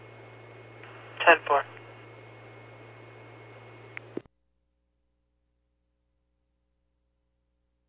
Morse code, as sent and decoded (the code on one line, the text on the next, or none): none